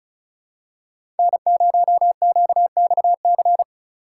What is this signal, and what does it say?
Morse code, heard as a keyed tone, about 36 wpm, 700 Hz. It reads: N0QXC
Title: USA Call Signs - 5 Characters - 2x Character Spacing - 35wpm